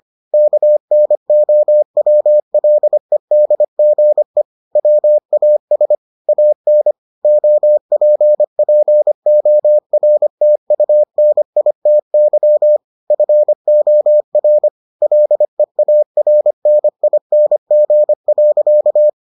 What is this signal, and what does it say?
Morse code, heard as a keyed tone, about 25 wpm, 600 Hz. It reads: KNOWLEDGE WAS AN OPPORTUNITY FOR LEARNING.